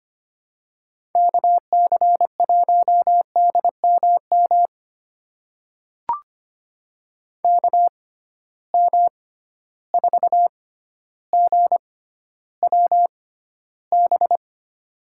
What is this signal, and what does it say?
Morse code, heard as a keyed tone, about 25 wpm, spaced as for 13 wpm, 700 Hz. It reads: KC1DMM KM4GWB
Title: USA Call Signs - 6x Character Spacing - 25wpm